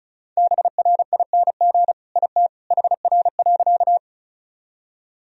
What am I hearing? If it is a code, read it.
Morse code, heard as a keyed tone, about 36 wpm, 700 Hz. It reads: BRING IT HR.
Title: Sentences from Top 500 Words and Abbreviations 35wpm